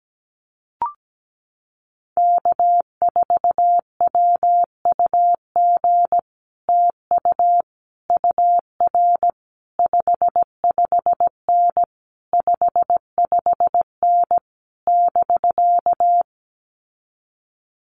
Morse code, heard as a keyed tone, about 17 wpm, 700 Hz. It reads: K4WUG TU UR 55N 55N <BK>
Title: Common QSO Phrases Encore 17wpm